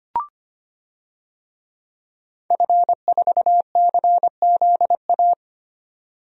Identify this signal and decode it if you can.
Morse code, heard as a keyed tone, about 25 wpm, 690 Hz. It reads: F4CZA
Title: DXCC-100 Call Signs - 5 Characters - 25wpm